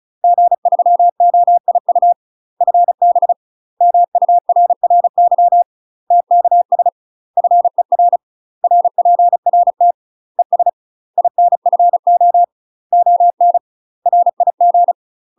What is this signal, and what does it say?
Morse code, heard as a keyed tone, about 35 wpm, 700 Hz. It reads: G3OIU FB MURRY TKS FER RPRT ES INFO ON RIG
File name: UK MASTER QSOs TRAINING / 35WPM / UK MASTER QSOs 0039